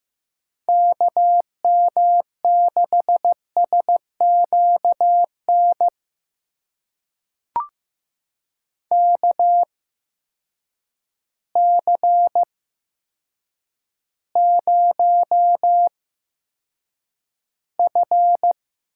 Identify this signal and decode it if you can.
Morse code, heard as a keyed tone, about 15 wpm, spaced as for 8 wpm, 700 Hz. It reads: KM6SQN KC0F
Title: USA Call Signs - 6 Characters - 8x Character Spacing - 15wpm